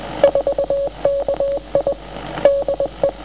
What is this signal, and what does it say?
Morse code, heard as a keyed tone, about 20 wpm, 540 Hz, with radio noise. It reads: 4KI DE